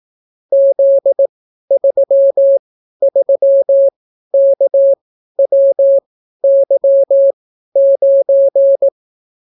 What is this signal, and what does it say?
Morse code, heard as a keyed tone, about 18 wpm, spaced as for 12 wpm, 550 Hz. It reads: Z33KWY9